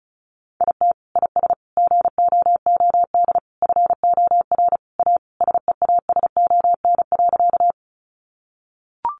Morse code, heard as a keyed tone, about 34 wpm, 700 Hz. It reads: IT IS GOOD FOR A SEASON.
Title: Sentences from Top 1400 Words 35wpm